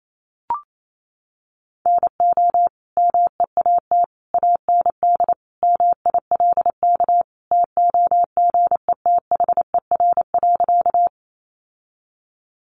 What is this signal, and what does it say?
Morse code, heard as a keyed tone, about 28 wpm, 700 Hz. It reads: NO MEAT AND MILK TOGETHER.
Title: Sentences from Top 1500 Words 28wpm